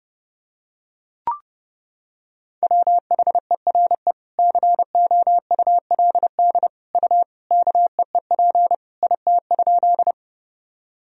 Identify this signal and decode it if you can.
Morse code, heard as a keyed tone, about 30 wpm, 700 Hz. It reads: WHERE COULD U KEEP IT?
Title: Sentences from Top 300 Words and Abbreviations 30wpm